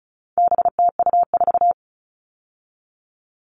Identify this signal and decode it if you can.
Morse code, heard as a keyed tone, about 34 wpm, 700 Hz. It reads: BTU4
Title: All Cut Numbers - 4-Digit 35wpm